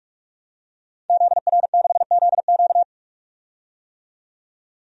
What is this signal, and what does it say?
Morse code, heard as a keyed tone, about 41 wpm, 700 Hz. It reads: ZR6ZQ
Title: DXCC-100 Call Signs - 5 Characters - 45wpm